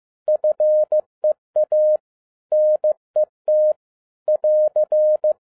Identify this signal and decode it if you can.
Morse code, heard as a keyed tone, about 15 wpm, 610 Hz. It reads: FEA NET <AR>
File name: 2012-01-mp3 / n17-SUMMARY OF FEA NET IN 2011-15wpm